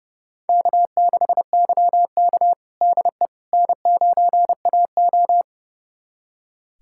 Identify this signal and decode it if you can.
Morse code, heard as a keyed tone, about 31 wpm, 700 Hz. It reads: K6YK DE N9AO
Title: USA Callsign DE Callsign - 4 Characters - 30wpm